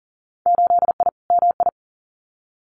Morse code, heard as a keyed tone, about 40 wpm, 700 Hz. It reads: 8I MI
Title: Winter Field Day Exchanges Encore 40wpm